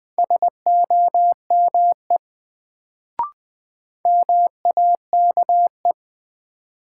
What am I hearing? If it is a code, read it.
Morse code, heard as a keyed tone, about 20 wpm, 700 Hz. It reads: SOME MAKE
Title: Sets of 2 Words - New Word Drill - 20wpm